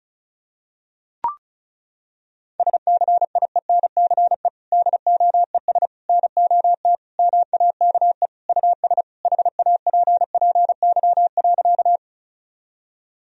Morse code, heard as a keyed tone, about 34 wpm, 700 Hz. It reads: SCIENCE DOES NOT MAKE US HAPPY.
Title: Sentences from Top 1000 Words - Review Part 2 - 35wpm